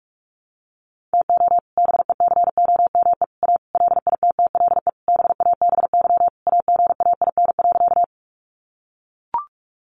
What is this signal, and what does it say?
Morse code, heard as a keyed tone, about 44 wpm, 700 Hz. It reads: TO BECOME A LITTLE BABY AGAIN.
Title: Sentences from Top 1300 Words 45wpm